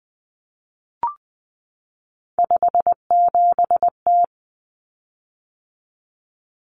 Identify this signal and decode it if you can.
Morse code, heard as a keyed tone, about 20 wpm, 700 Hz. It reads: 57T
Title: Common Cut Numbers - 3-Digit 20wpm